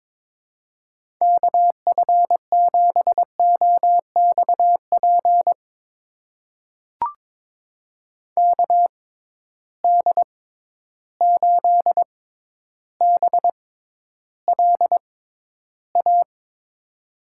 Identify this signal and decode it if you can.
Morse code, heard as a keyed tone, about 21 wpm, spaced as for 11 wpm, 700 Hz. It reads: KF7OXP KD8BLA
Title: USA Call Signs - 6 Characters - 6x Character Spacing - 22wpm